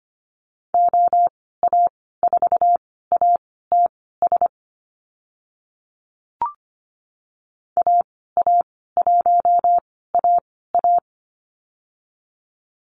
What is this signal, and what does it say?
Morse code, heard as a keyed tone, about 25 wpm, spaced as for 16 wpm, 700 Hz. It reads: OA4ATS AA1AA
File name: Lesson 15g - Number 5 - Call Signs 25wpm